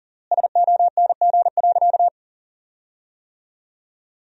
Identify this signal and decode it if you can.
Morse code, heard as a keyed tone, about 40 wpm, 700 Hz. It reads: SONG.
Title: 3x Word-Spacing - Sentences from Top 400 Words - 40wpm